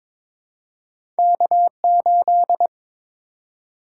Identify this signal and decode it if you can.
Morse code, heard as a keyed tone, about 22 wpm, 700 Hz. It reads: K8
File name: practice / 22wpm / Two Letter-Number Combinations - 22wpm